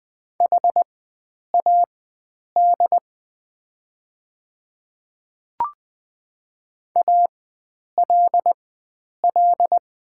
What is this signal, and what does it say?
Morse code, heard as a keyed tone, about 20 wpm, spaced as for 9 wpm, 700 Hz. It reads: HAD ALL
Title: 4x Character Spacing - Top 100 Words - 20wpm